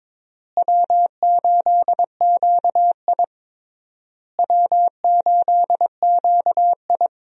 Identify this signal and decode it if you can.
Morse code, heard as a keyed tone, about 21 wpm, 700 Hz. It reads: W8QI W8QI
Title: USA Call Signs - 4 Characters - Repeated 3x - 22wpm